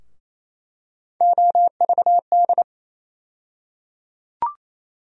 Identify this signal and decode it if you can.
Morse code, heard as a keyed tone, about 28 wpm, 700 Hz. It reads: OVD
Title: USA Call Sign Postfixes - 3 Letters 28wpm